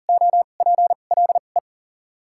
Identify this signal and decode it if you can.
Morse code, heard as a keyed tone, about 40 wpm, spaced as for 29 wpm, 700 Hz. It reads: OPLE
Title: 2x Character Spacing - Top 500 Words - 40wpm